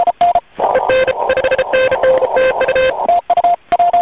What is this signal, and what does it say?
Morse code, heard as a keyed tone, about 33 wpm, 520 Hz, with noise. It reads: EN 5NN TU
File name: j7 vp2m 2002 / Voice / 12 001 2002 10 01 j75ea